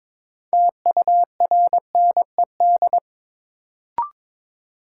Morse code, heard as a keyed tone, about 22 wpm, 700 Hz. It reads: TURNED
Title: Sets of 4 Words - New Word Drill - 22wpm